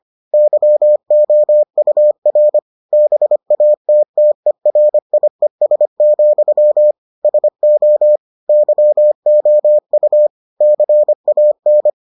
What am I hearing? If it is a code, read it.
Morse code, heard as a keyed tone, about 25 wpm, 600 Hz. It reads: YOUR BATTERIES, SO YOU CAN